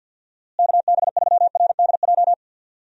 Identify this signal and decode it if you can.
Morse code, heard as a keyed tone, about 52 wpm, 700 Hz. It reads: KB3RDJ
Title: USA Call Signs - 6 Characters - Repeated 2x - 50wpm